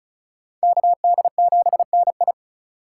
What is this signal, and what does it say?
Morse code, heard as a keyed tone, about 45 wpm, 700 Hz. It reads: KD7NI